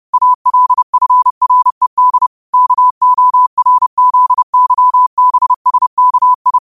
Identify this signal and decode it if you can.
Morse code, heard as a keyed tone, about 30 wpm, 1000 Hz. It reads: ALFRED KORZYBSKI
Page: New quotes every day in morse code at 30 Words per minute.